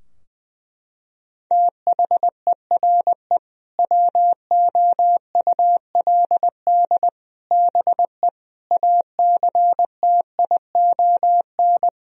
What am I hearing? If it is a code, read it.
Morse code, heard as a keyed tone, about 21 wpm, 700 Hz. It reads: THERE WOULD BE ACTION
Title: Sentences from Top 1000 Words - Review Part 4 - 20wpm